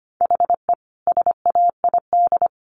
Morse code, heard as a keyed tone, about 25 wpm, 700 Hz. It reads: HE SAID
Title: Sentences from Top 1500 Words 25wpm